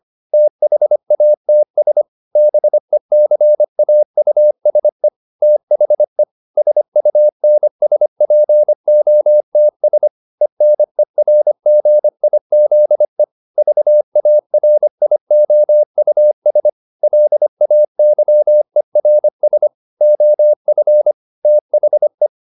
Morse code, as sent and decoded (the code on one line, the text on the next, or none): - .... .- - ... / -... -.-. .- ..- ... / - .... / ... ..- -. ... .--. --- - ... / . -. .-. --. .. --.. / ...- .- .-. .. --- ..- ... / .-.. .- -.-- . .-. ... / --- ..-. / - .... .
THATS BECAUSE THE SUNSPOTS ENERGIZE VARIOUS LAYERS OF THE